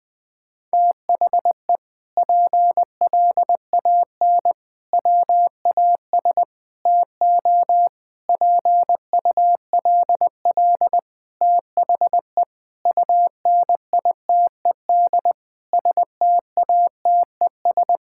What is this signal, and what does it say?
Morse code, heard as a keyed tone, about 20 wpm, 700 Hz. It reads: THE PLAN WAS TO PULL THE UNITED STATES
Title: Sentences with 8-Word Phrases - 20wpm